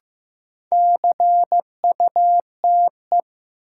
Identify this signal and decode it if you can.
Morse code, heard as a keyed tone, about 15 wpm, 700 Hz. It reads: CUTE
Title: Sentences from Top 1000 Words - Review Part 7 - New Word Drill - 15wpm